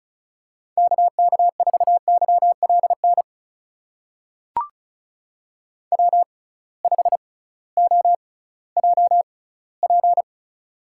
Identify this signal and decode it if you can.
Morse code, heard as a keyed tone, about 36 wpm, spaced as for 19 wpm, 700 Hz. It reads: KK4YLN W5OJP
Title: USA Call Signs - 6x Character Spacing - 35wpm